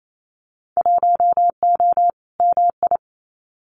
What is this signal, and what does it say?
Morse code, heard as a keyed tone, about 28 wpm, 700 Hz. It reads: 1O MI
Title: Winter Field Day Exchanges Encore 28wpm